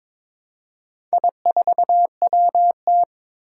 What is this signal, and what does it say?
Morse code, heard as a keyed tone, about 22 wpm, 700 Hz. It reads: I4WT
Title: Four Challenging Letter-Number Combinations 22wpm